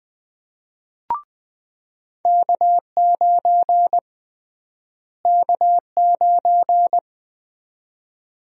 Morse code, heard as a keyed tone, about 21 wpm, 700 Hz. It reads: K9 K9
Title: USA Call Sign Prefixes - Repeated 2x - 20wpm